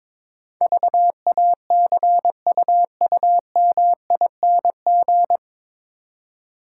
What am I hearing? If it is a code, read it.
Morse code, heard as a keyed tone, about 22 wpm, 700 Hz. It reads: VACUUMING
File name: Easy Words - Speed-Racing with Hints - 22wpm